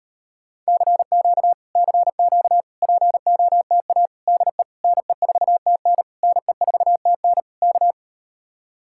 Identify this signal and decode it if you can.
Morse code, heard as a keyed tone, about 39 wpm, 700 Hz. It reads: CQ CQ POTA DE NE4TN NE4TN K